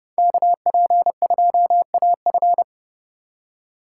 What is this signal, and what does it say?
Morse code, heard as a keyed tone, about 31 wpm, 700 Hz. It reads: KP2AF
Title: USA Call Signs - 5 Characters - 3x Character Spacing - 30wpm